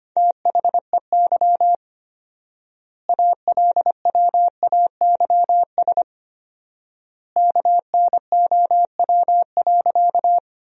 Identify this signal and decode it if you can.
Morse code, heard as a keyed tone, about 25 wpm, 700 Hz. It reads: THEY ALWAYS KNOW.
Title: 4x Word-Spacing - Sentences from Top 300 Words - 25wpm